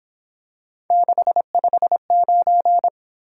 Morse code, heard as a keyed tone, about 26 wpm, 700 Hz. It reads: 659
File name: Speed-Racing - Three Letter-Number Combinations - 22wpm